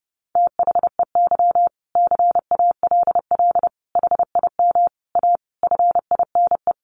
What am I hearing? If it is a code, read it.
Morse code, heard as a keyed tone, about 30 wpm, 700 Hz. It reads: THEY CALL HIM A FINE
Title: Sentences from Top 500 Words 30wpm